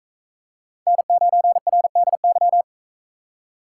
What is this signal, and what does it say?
Morse code, heard as a keyed tone, about 42 wpm, 700 Hz. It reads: N9RDY